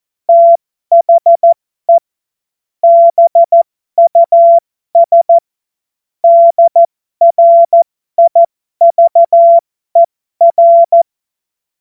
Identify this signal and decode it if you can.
Morse code, heard as a keyed tone, about 14 wpm, spaced as for 12 wpm, 680 Hz. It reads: THE BUS DRIVER
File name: Morsekurs / Witze